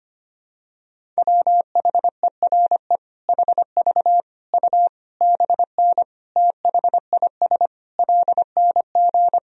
Morse code, heard as a keyed tone, about 25 wpm, 700 Hz. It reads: WHERE HV U BN THIS LNG
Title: Sentences from Top 300 Words and Abbreviations 25wpm